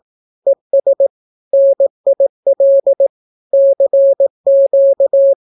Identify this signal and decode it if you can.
Morse code, heard as a keyed tone, about 18 wpm, 550 Hz. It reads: ES NIL CQ
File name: naukapodstaw / skroty / skroty